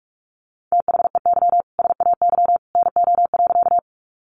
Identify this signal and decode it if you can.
Morse code, heard as a keyed tone, about 44 wpm, 700 Hz. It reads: THEY SAY NO.
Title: Easy Phrases 45wpm